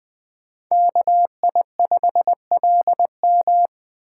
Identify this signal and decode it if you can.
Morse code, heard as a keyed tone, about 20 wpm, 700 Hz. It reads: KI5LM
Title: Speed Racing - USA 2x2 Call Signs 17wpm